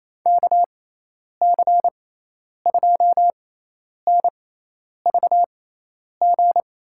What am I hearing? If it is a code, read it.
Morse code, heard as a keyed tone, about 28 wpm, spaced as for 10 wpm, 700 Hz. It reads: KC2NVG
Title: USA Call Signs - 6x Character Spacing - 28wpm